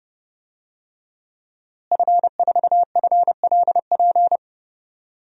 Morse code, heard as a keyed tone, about 30 wpm, 700 Hz. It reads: F4FLP
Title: DXCC-100 Call Signs - 5 Characters - Repeated 3x - 30wpm